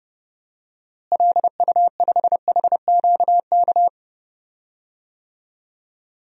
Morse code, heard as a keyed tone, about 31 wpm, 700 Hz. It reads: LU5HQK
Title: DXCC-100 Call Signs - 6 Characters - Repeated 2x - 30wpm